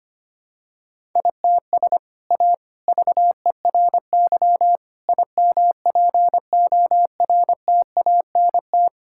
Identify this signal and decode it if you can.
Morse code, heard as a keyed tone, about 25 wpm, 700 Hz. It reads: ITS A VERY IMPORTANT
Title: Sentences from Top 1000 Words - Review Part 5 - 25wpm